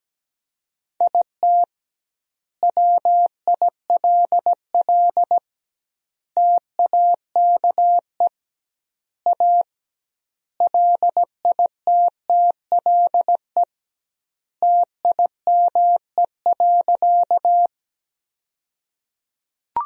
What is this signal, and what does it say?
Morse code, heard as a keyed tone, about 17 wpm, 700 Hz. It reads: IT WILL TAKE A LITTLE TIME.
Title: 2x Word-Spacing - Sentences from Top 200 Words - 17wpm